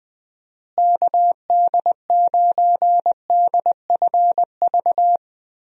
Morse code, heard as a keyed tone, about 20 wpm, 700 Hz. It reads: KD9DFV
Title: USA Call Signs - 6 Characters - Repeated 3x - 20wpm